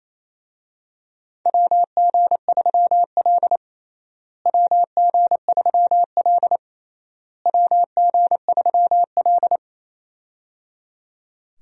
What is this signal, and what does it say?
Morse code, heard as a keyed tone, about 28 wpm, 700 Hz. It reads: WG3L WG3L WG3L
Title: USA Call Signs - 4 Characters - Repeated 3x - 28wpm